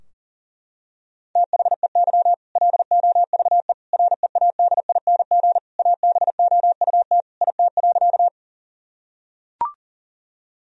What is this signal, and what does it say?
Morse code, heard as a keyed tone, about 40 wpm, 700 Hz. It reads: THEY LOVE READING ABOUT IT.
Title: Sentences from Top 1000 Words - Review Part 3 - 40wpm